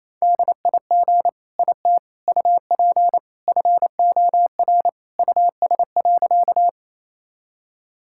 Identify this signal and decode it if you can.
Morse code, heard as a keyed tone, about 28 wpm, 700 Hz. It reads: DIG IT UP FOR US.
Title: Sentences from Top 1600 Words 28wpm